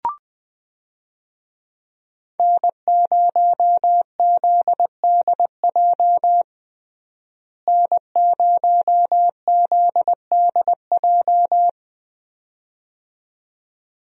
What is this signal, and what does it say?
Morse code, heard as a keyed tone, about 20 wpm, 700 Hz. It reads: N0ZDJ N0ZDJ
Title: USA Call Signs - 5 Characters - Repeated 2x - 20wpm